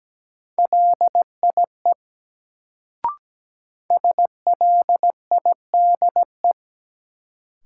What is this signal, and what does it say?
Morse code, heard as a keyed tone, about 17 wpm, 700 Hz. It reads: LIE SLIDE
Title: Sentences from Top 1600 Words - New Word Drill Alt - 17wpm